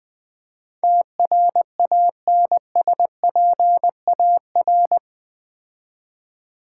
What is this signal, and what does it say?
Morse code, heard as a keyed tone, about 20 wpm, 700 Hz. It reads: TRANSPAR
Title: Build A Word - 4 Letters At A Time - 12-Letter Words - 20wpm